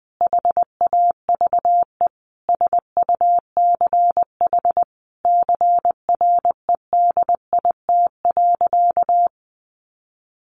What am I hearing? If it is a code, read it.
Morse code, heard as a keyed tone, about 20 wpm, 700 Hz. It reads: HAVE SUCH CREDIT.
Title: Sentences from Top 1000 Words 20wpm